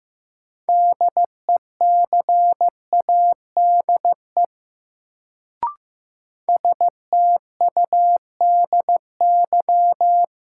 Morse code, heard as a keyed tone, about 15 wpm, 700 Hz. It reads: DECADE STUDY
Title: Sets of 5 Words - New Word Drill - 15wpm